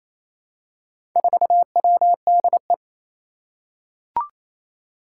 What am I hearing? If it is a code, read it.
Morse code, heard as a keyed tone, about 28 wpm, 700 Hz. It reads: 4WDE
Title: Four Challenging Letter-Number Combinations 28wpm